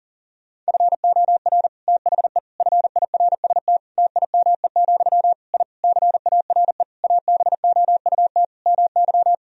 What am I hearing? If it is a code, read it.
Morse code, heard as a keyed tone, about 40 wpm, 700 Hz. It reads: FOR THE FIRST TIME, I CARE ABOUT MY
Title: Sentences from Top 1000 Words - Review Part 2 - 40wpm